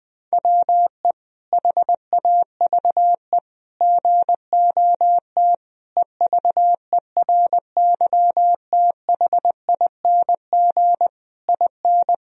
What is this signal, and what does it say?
Morse code, heard as a keyed tone, about 20 wpm, 700 Hz. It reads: WE HAVE GOT EVERYTHING IN
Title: Sentences from Top 1000 Words - Review Part 3 - 20wpm